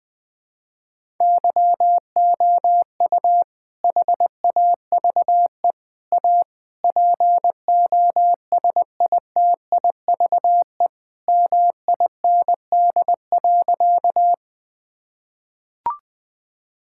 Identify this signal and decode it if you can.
Morse code, heard as a keyed tone, about 21 wpm, 700 Hz. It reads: YOU HAVE A POSITIVE MIND.
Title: Sentences from Top 1000 Words - Review Part 5 - 20wpm